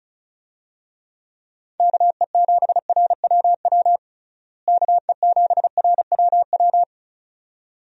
Morse code, heard as a keyed tone, about 36 wpm, 700 Hz. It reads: KE7RWW KE7RWW
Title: USA Call Signs - 6 Characters - Repeated 2x - 35wpm